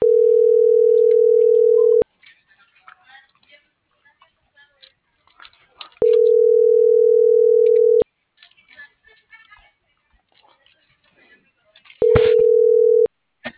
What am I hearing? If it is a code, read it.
Ringback tone (North America), 440 + 480 Hz.